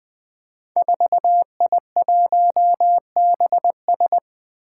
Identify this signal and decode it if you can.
Morse code, heard as a keyed tone, about 20 wpm, 700 Hz. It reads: 4I1BS
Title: Five Challenging Letter-Number Combinations 20wpm